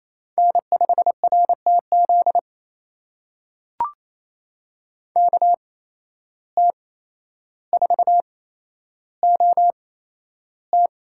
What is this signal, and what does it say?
Morse code, heard as a keyed tone, about 28 wpm, spaced as for 12 wpm, 700 Hz. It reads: N5RTZ KT4OT
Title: USA Call Signs - 5 Characters - 8x Character Spacing - 28wpm